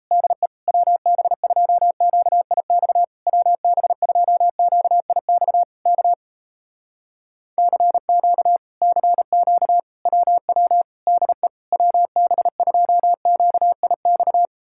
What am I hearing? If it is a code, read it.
Morse code, heard as a keyed tone, about 38 wpm, 700 Hz. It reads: DE WB2QIX WB2QIX K CQ CQ WW DE WB2QIX